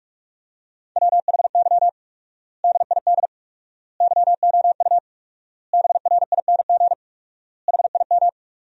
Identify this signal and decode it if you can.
Morse code, heard as a keyed tone, about 45 wpm, 700 Hz. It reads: WHY DID YOU BRING HIM
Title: 4x Word-Spacing - Sentences from Top 500 Words - 45wpm